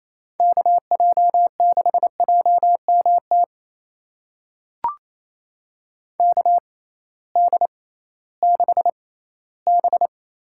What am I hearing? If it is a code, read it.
Morse code, heard as a keyed tone, about 28 wpm, spaced as for 16 wpm, 700 Hz. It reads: KJ6JMT KD6B
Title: USA Call Signs - 6 Characters - 6x Character Spacing - 28wpm